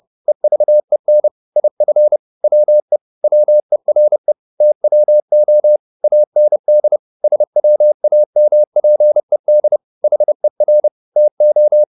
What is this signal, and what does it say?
Morse code, heard as a keyed tone, about 30 wpm, 600 Hz. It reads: EVEN IF WE WERE TWO AND SWAMPED HER TO